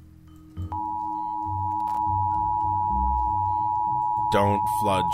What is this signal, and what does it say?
Emergency Alert System (SAME) tones (the 853 and 960 Hz attention tone), with speech. It is heard from 0.7 to 4.3 s.